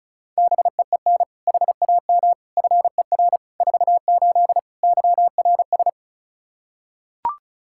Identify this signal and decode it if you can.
Morse code, heard as a keyed tone, about 36 wpm, 700 Hz. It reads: BEEN HAM FER 48 YRS
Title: Practice QSO - Exchange 3 - More Info - 35wpm